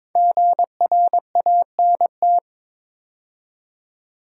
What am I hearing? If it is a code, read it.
Morse code, heard as a keyed tone, about 22 wpm, 700 Hz. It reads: GRANT
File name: 5 Letter Words - First 3 Letters Given - 22wpm